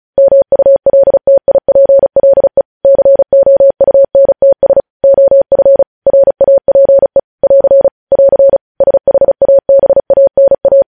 Morse code, heard as a keyed tone, about 36 wpm, 550 Hz. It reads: MULTIPLE COUNTS OF RAPE <AR> <AR> SHABANA
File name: podcast / News headlines-1x-35x35